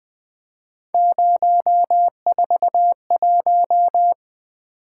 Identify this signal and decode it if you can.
Morse code, heard as a keyed tone, about 21 wpm, 700 Hz. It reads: 041
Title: Three Letter-Number Combinations - 20wpm